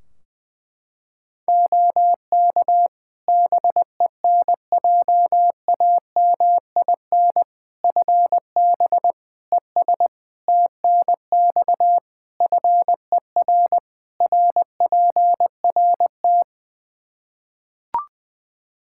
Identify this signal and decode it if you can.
Morse code, heard as a keyed tone, about 20 wpm, 700 Hz. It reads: OK BENJAMIN FB ES TNX FER RPRT
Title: K1USN SST QSO Practice 20wpm